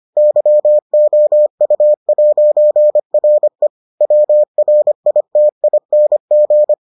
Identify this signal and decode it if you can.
Morse code, heard as a keyed tone, about 25 wpm, 600 Hz. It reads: YOU'RE WRITING